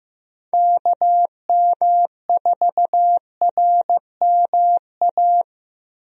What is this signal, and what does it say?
Morse code, heard as a keyed tone, about 15 wpm, 700 Hz. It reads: KM4RMA
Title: USA Call Signs - 6 Characters - Repeated 2x - 15wpm